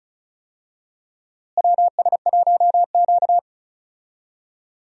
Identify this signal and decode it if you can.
Morse code, heard as a keyed tone, about 36 wpm, 700 Hz. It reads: WS1Q
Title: USA Call Signs - 4 Characters - Repeated 3x - 35wpm